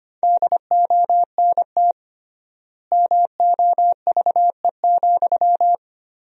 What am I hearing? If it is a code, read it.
Morse code, heard as a keyed tone, about 25 wpm, 700 Hz. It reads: DONT MOVE,
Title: 3x Word-Spacing - Sentences from Top 300 Words - 25wpm